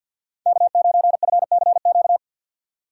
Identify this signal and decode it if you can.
Morse code, heard as a keyed tone, about 51 wpm, 700 Hz. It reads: K9FCQ
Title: USA Call Signs - 3x Character Spacing - 50wpm